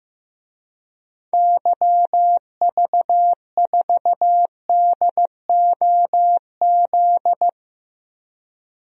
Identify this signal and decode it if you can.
Morse code, heard as a keyed tone, about 15 wpm, 700 Hz. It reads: YV4DOZ
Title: DXCC-100 Call Signs - 6 Characters - 15wpm